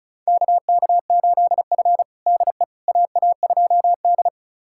Morse code, heard as a keyed tone, about 36 wpm, 700 Hz. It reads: KK8F DE AA2D
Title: USA Callsign DE Callsign - 4 Characters - 35wpm